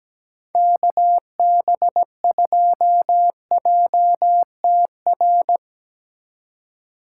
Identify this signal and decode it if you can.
Morse code, heard as a keyed tone, about 18 wpm, 700 Hz. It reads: KB2JTR
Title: Speed Racing - USA Call Signs 17wpm